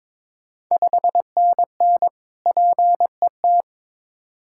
Morse code, heard as a keyed tone, about 22 wpm, 690 Hz. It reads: 5NN PET
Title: Ontario QSO Party Exchange 22wpm